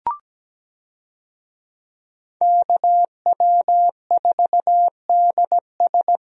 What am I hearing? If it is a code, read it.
Morse code, heard as a keyed tone, about 17 wpm, 700 Hz. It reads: KW4DS
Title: USA Call Signs - 5 Characters - Repeated 2x - 17wpm